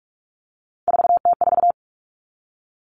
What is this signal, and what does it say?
Morse code, heard as a keyed tone, about 44 wpm, 700 Hz. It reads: 4T4
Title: Common Cut Numbers - 3-Digit 45wpm